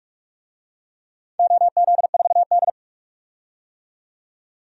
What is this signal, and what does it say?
Morse code, heard as a keyed tone, about 45 wpm, 700 Hz. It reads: OZ4D
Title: DXCC-100 Call Signs - 4 Characters - 45wpm